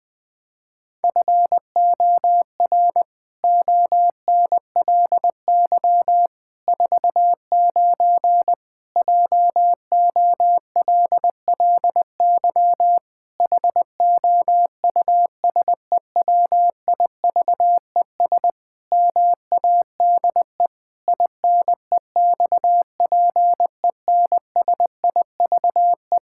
Morse code, heard as a keyed tone, about 20 wpm, 700 Hz. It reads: FOR ONLY 49 JOLLY HOUSEWIVES MADE INEXPENSIVE
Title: Pangram Challenge v2 - 20wpm